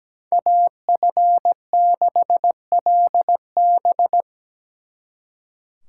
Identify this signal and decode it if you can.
Morse code, heard as a keyed tone, about 20 wpm, 700 Hz. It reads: AF6LB
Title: Speed Racing - USA 2x2 Call Signs 17wpm